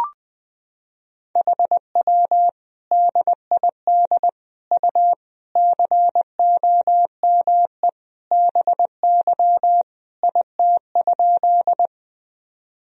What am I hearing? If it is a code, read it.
Morse code, heard as a keyed tone, about 20 wpm, 700 Hz. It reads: HW DID U COME BY IT?
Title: Sentences from Top 100 Words and Abbreviations 20wpm